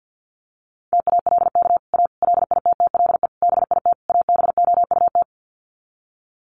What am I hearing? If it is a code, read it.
Morse code, heard as a keyed tone, about 50 wpm, 700 Hz. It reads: TALK A LITTLE BIT ABOUT
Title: Sets of 5 Words Encore 50wpm